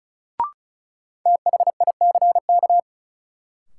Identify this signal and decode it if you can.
Morse code, heard as a keyed tone, about 34 wpm, 700 Hz. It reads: THICK
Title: Sentences from Top 1600 Words - New Word Drill - 35wpm